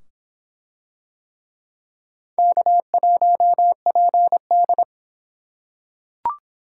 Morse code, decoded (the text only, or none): K1PD